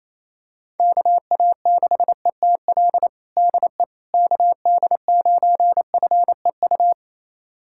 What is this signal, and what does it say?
Morse code, heard as a keyed tone, about 28 wpm, 700 Hz. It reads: KA6ETL DE KD9FEU
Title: USA Callsign DE Callsign - 6 Characters - 28wpm